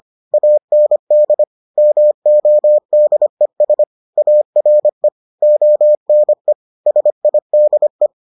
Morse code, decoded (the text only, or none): AND MODES ARE ONE SIDE